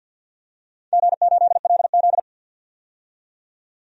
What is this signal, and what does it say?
Morse code, heard as a keyed tone, about 52 wpm, 700 Hz. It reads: G8LZ
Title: DXCC-100 Call Signs - 4 Characters - 50wpm